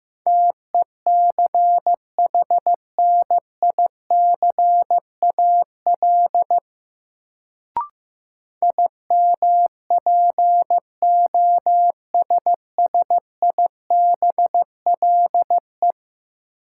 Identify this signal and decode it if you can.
Morse code, heard as a keyed tone, about 15 wpm, 700 Hz. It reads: TECHNICAL IMPOSSIBLE
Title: Sentences from Top 1100 Words - New Word Drill - 15wpm